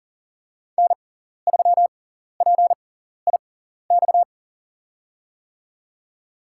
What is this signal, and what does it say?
Morse code, heard as a keyed tone, about 40 wpm, spaced as for 14 wpm, 700 Hz. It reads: N3PIX
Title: USA Call Signs - 5 Characters - 6x Character Spacing - 40wpm